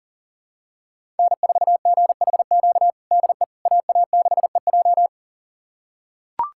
Morse code, heard as a keyed tone, about 37 wpm, 700 Hz. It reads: N4GHQ DE AA6EJ
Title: USA Callsign DE Callsign - 5 Characters - 40wpm